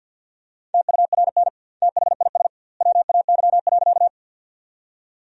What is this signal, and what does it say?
Morse code, heard as a keyed tone, about 50 wpm, 700 Hz. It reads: TURN THIS WAY.
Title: 2x Word-Spacing - Sentences from Top 200 Words - 50wpm